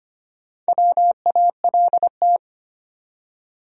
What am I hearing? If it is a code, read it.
Morse code, heard as a keyed tone, about 25 wpm, 700 Hz. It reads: WALT
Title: USA Common Names - Max 5 Letters - 25wpm